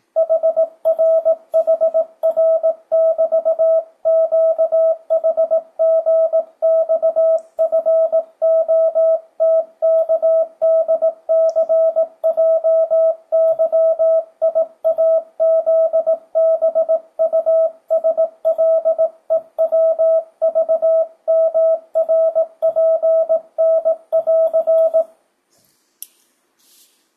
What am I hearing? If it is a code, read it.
Morse code, heard as a keyed tone, about 18 wpm, 650 Hz. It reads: HRHR<BT>QHGXFOTKDCJYIAZBUSLEWVMRPN<AR>
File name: 2025cw / speedup / 18wpm